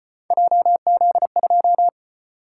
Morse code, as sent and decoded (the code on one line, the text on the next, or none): .--- --.. ..---
JZ2